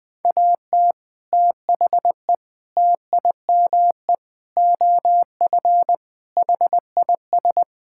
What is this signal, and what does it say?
Morse code, heard as a keyed tone, about 21 wpm, 700 Hz. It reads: AT THE TIME OF HIS
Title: Sets of 6 Words - 20wpm